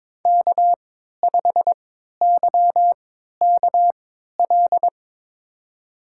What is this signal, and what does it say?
Morse code, heard as a keyed tone, about 22 wpm, spaced as for 13 wpm, 700 Hz. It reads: K5YKL
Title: USA Call Signs - 3x Character Spacing - 22wpm